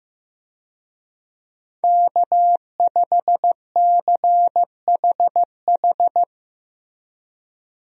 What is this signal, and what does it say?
Morse code, heard as a keyed tone, about 15 wpm, 700 Hz. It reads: K5CHH
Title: USA Call Signs - 5 Characters - Repeated 3x - 15wpm